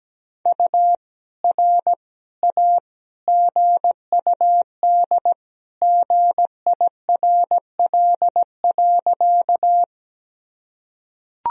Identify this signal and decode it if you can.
Morse code, heard as a keyed tone, about 17 wpm, 700 Hz. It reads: U R A GUD GIRL.
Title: Sentences from Top 400 Words and Abbreviations 17wpm